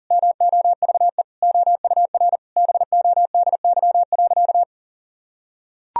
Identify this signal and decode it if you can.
Morse code, heard as a keyed tone, about 40 wpm, 700 Hz. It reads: MOVE OUR BODY.
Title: Sentences from Top 1000 Words - Review Part 2 - 40wpm